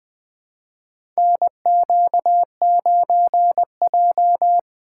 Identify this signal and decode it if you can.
Morse code, heard as a keyed tone, about 20 wpm, 700 Hz. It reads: NQ9J
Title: USA Call Signs - 4 Characters - 3x Character Spacing - 20wpm